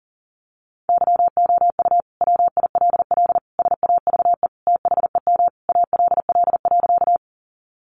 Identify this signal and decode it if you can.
Morse code, heard as a keyed tone, about 40 wpm, 700 Hz. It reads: YOU WILL SAVE THEM ALL.
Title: Sentences from Top 1100 Words 40wpm